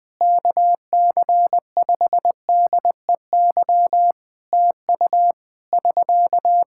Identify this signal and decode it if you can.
Morse code, heard as a keyed tone, about 21 wpm, 700 Hz. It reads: KC5DEY TU <SK>
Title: Practice QSO - Exchange 4 - Ending - 20wpm